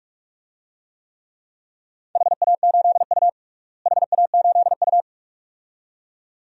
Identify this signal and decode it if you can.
Morse code, heard as a keyed tone, about 41 wpm, 700 Hz. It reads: HA8U HA8U
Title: DXCC-100 Call Signs - 4 Characters - Repeated 2x - 45wpm